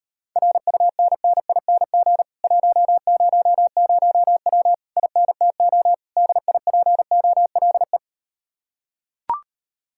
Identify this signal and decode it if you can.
Morse code, heard as a keyed tone, about 39 wpm, 700 Hz. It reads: RUNNING 100W INTO DIPOLE